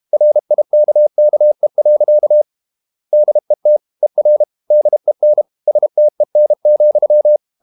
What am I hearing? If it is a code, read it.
Morse code, heard as a keyed tone, about 32 wpm, 600 Hz. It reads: RIKKE. DET ER DEN STEN,
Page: Ferie 32 wpm